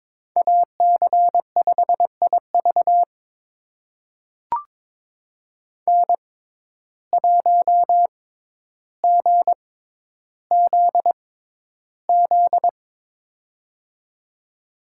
Morse code, heard as a keyed tone, about 22 wpm, spaced as for 11 wpm, 700 Hz. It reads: AC5IV N1GZZ
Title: USA Call Signs - 5 Characters - 6x Character Spacing - 22wpm